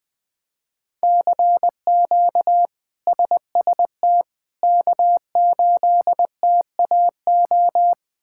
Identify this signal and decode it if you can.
Morse code, heard as a keyed tone, about 20 wpm, 700 Hz. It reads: CQ SST K8TAO
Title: K1USN SST QSO Practice - 20wpm